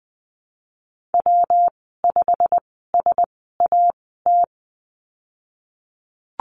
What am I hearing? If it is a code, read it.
Morse code, heard as a keyed tone, about 20 wpm, spaced as for 15 wpm, 700 Hz. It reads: W5SAT
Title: 147 Lesson 24g - Letter y - Call Signs 20wpm